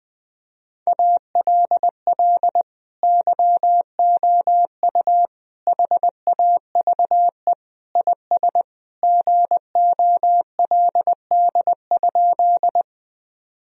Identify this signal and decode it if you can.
Morse code, heard as a keyed tone, about 21 wpm, 700 Hz. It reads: ALL YOU HAVE IS GOLD?
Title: Sentences from Top 1000 Words - Review Part 3 - 20wpm